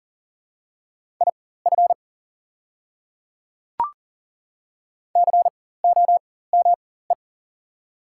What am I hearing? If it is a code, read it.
Morse code, heard as a keyed tone, about 40 wpm, spaced as for 19 wpm, 700 Hz. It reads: IF COME
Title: 4x Character Spacing - Top 100 Words - 40wpm